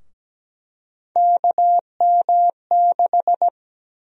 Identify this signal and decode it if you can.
Morse code, heard as a keyed tone, about 17 wpm, 700 Hz. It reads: KM6
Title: USA Call Sign Prefixes - 17wpm